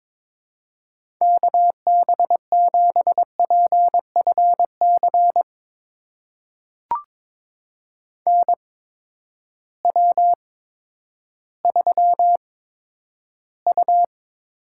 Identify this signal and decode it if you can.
Morse code, heard as a keyed tone, about 22 wpm, spaced as for 11 wpm, 700 Hz. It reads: KB7PFC NW3U
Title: USA Call Signs - 8x Character Spacing - 22wpm